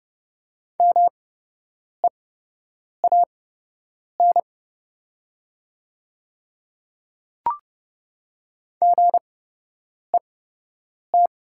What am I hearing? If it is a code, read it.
Morse code, heard as a keyed tone, about 30 wpm, spaced as for 8 wpm, 700 Hz. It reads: MEAN GET
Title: 8x Character Spacing - Top 500 Words - 30wpm